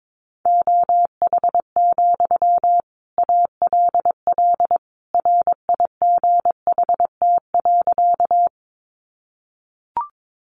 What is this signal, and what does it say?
Morse code, heard as a keyed tone, about 22 wpm, 700 Hz. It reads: OH, ALL RIGHT.
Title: Sentences from Top 500 Words 22wpm